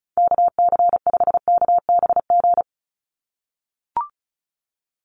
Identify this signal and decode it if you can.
Morse code, heard as a keyed tone, about 36 wpm, 700 Hz. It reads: KC5KBG
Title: USA Call Signs 35wpm